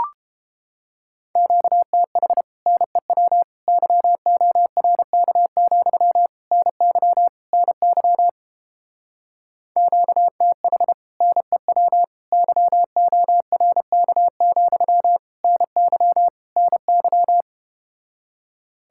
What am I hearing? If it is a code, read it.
Morse code, heard as a keyed tone, about 29 wpm, 700 Hz. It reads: QTH NEW YORK, NY NY QTH NEW YORK, NY NY